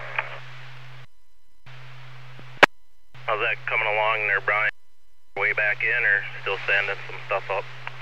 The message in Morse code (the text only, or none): none